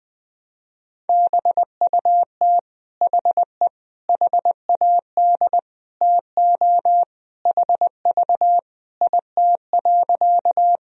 Morse code, heard as a keyed tone, about 21 wpm, 700 Hz. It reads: BUT HE HAD TO HV IT.
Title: Sentences from Top 100 Words and Abbreviations 20wpm